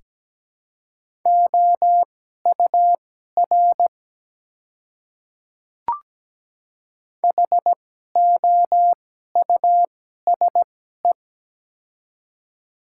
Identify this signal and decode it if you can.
Morse code, heard as a keyed tone, about 17 wpm, spaced as for 12 wpm, 700 Hz. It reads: OUR HOUSE
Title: 2x Character Spacing - Top 200 Words - 17wpm